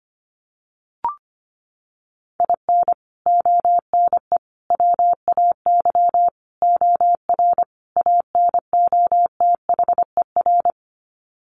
Morse code, heard as a keyed tone, about 25 wpm, 690 Hz. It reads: IN ONE WAY OR ANOTHER
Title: Sets of 5 Words Encore 25wpm